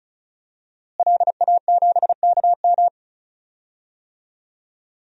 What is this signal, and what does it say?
Morse code, heard as a keyed tone, about 34 wpm, 700 Hz. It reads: LA7KM
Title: DXCC-100 Call Signs - 5 Characters - Repeated 2x - 35wpm